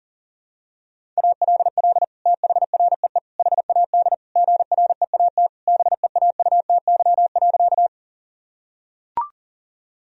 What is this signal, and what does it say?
Morse code, heard as a keyed tone, about 37 wpm, 700 Hz. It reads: ALL THREE HAD GREAT BEAUTY.
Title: Sentences from Top 1000 Words - Review Part 2 - 40wpm